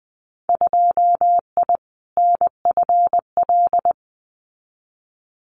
Morse code, heard as a keyed tone, about 20 wpm, 700 Hz. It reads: 2I NFL
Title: Winter Field Day Exchanges 20wpm